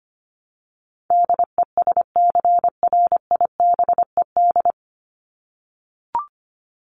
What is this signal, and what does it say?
Morse code, heard as a keyed tone, about 25 wpm, 700 Hz. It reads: DESCRIBED
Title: ED Suffix - Top 100 Words with Base Words 25wpm